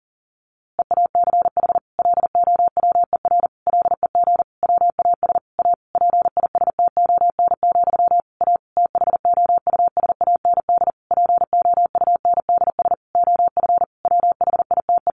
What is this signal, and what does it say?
Morse code, heard as a keyed tone, about 40 wpm, 700 Hz. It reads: EACH LOWER LEG WAS A PISTON, A THOUSAND POUNDS OF WHITE
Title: A Sound Of Thunder 40wpm